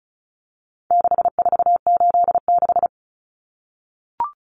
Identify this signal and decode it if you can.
Morse code, heard as a keyed tone, about 34 wpm, 700 Hz. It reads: 6486
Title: Numbers - 4-Digit 35wpm